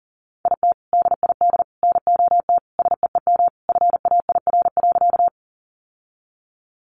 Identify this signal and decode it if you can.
Morse code, heard as a keyed tone, about 40 wpm, 700 Hz. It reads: IT DID NOT SEEM FAIR.
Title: Sentences from Top 1300 Words 40wpm